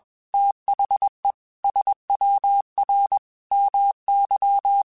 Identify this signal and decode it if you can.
Morse code, heard as a keyed tone, about 21 wpm, 800 Hz. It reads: THE SWR MY
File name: QSOs / QSOs-mp3s / WD0FBT 1@20